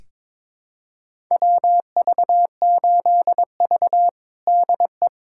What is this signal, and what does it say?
Morse code, heard as a keyed tone, about 22 wpm, 700 Hz. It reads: WV8V DE
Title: USA Callsign DE Callsign - 4 Characters - 22wpm